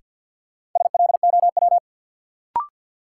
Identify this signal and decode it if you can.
Morse code, heard as a keyed tone, about 49 wpm, 700 Hz. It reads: SLOW
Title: Sentences from Top 400 Words - New Word Drill - 50wpm